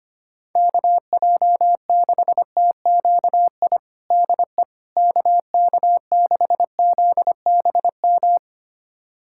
Morse code, heard as a keyed tone, about 25 wpm, 700 Hz. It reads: KJ6TQI DE KK6ZBM
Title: USA Callsign DE Callsign 25wpm